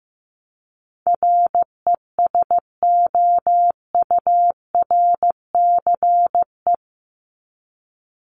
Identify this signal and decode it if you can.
Morse code, heard as a keyed tone, about 15 wpm, 700 Hz. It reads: RESOURCE
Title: ES:S Suffix - Top 100 Words with Base Words 15wpm